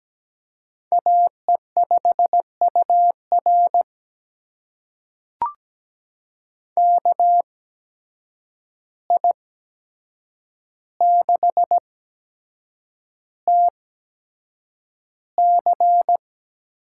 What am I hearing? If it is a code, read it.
Morse code, heard as a keyed tone, about 17 wpm, spaced as for 7 wpm, 700 Hz. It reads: AE5UR KI6TC
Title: USA Call Signs - 5 Characters - 8x Character Spacing - 17wpm